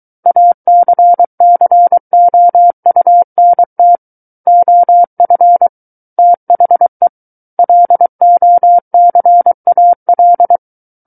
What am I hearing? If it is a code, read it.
Morse code, heard as a keyed tone, about 23 wpm, 700 Hz. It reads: ACCOUNT OF THE LOCAL
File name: Book One / 14-Chapter 14 - 23 WPM